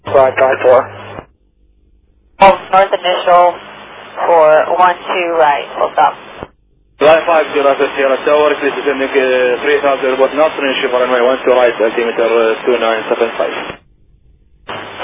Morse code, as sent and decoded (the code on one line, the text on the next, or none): none